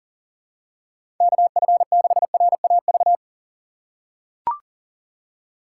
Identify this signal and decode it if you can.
Morse code, heard as a keyed tone, about 40 wpm, 700 Hz. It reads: KF6RAV
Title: USA Call Signs - 6 Characters - 4x Character Spacing - 40wpm